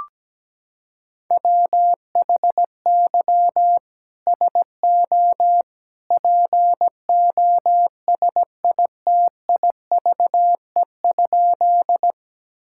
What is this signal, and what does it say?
Morse code, heard as a keyed tone, about 17 wpm, 700 Hz. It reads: WHY SO POSITIVE?
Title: Sentences from Top 1000 Words - Review Part 5 - 17wpm